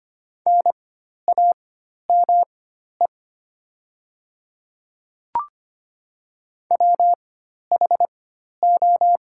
Morse code, heard as a keyed tone, about 25 wpm, spaced as for 12 wpm, 700 Hz. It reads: NAME WHO
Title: 4x Character Spacing - Top 200 Words - 25wpm